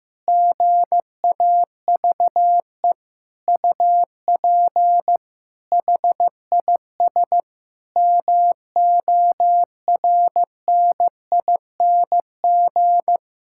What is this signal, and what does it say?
Morse code, heard as a keyed tone, about 15 wpm, 700 Hz. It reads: GAVE UP HIS MORNING
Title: Sentences from Top 1000 Words - Review Part 5 - 15wpm